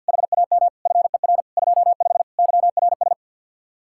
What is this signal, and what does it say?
Morse code, heard as a keyed tone, about 59 wpm, 700 Hz. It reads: HAM FER 25 YRS
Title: Speed-Racing Common QSO Phrases - 50wpm